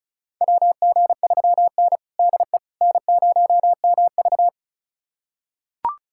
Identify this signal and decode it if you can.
Morse code, heard as a keyed tone, about 36 wpm, 700 Hz. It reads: WG3N DE N0MV
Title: USA Callsign DE Callsign - 4 Characters - 35wpm